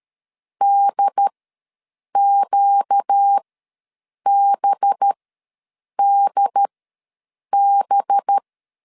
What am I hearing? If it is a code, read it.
Morse code, heard as a keyed tone, about 13 wpm, spaced as for 7 wpm, 800 Hz. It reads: DQBDB
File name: 5 LB ws 12wpm